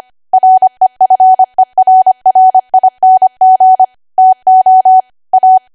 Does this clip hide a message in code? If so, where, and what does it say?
Yes: Morse code from 0.3 to 5.6 s, heard as a keyed tone, about 25 wpm, 750 Hz. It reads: REFERRING TO A